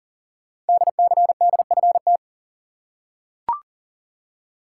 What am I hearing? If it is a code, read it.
Morse code, heard as a keyed tone, about 39 wpm, 700 Hz. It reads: DCDFT
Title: Code Groups - 5 Letters - 40wpm